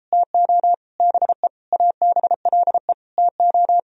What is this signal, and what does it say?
Morse code, heard as a keyed tone, about 32 wpm, 700 Hz. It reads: TO BE ABLE TO
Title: Speed-Racing - Sets of 5 Words 28wpm